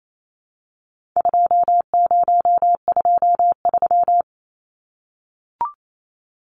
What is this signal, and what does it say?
Morse code, heard as a keyed tone, about 28 wpm, 700 Hz. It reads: 2023
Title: Numbers - 4-Digit 28wpm